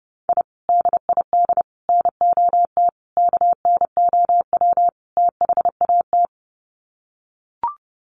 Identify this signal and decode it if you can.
Morse code, heard as a keyed tone, about 30 wpm, 700 Hz. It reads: I DID NOT KNOW THAT
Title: Sets of 5 Words 30wpm